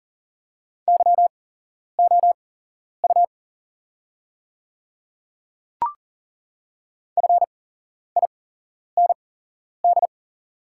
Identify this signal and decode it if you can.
Morse code, heard as a keyed tone, about 39 wpm, spaced as for 11 wpm, 690 Hz. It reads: YOU FIND
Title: 8x Character Spacing - Top 100 Words - 40wpm